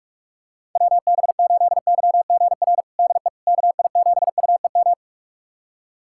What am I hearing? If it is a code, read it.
Morse code, heard as a keyed tone, about 47 wpm, 690 Hz. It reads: WB8YGR DE KI7UEM
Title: USA Callsign DE Callsign - 6 Characters - 45wpm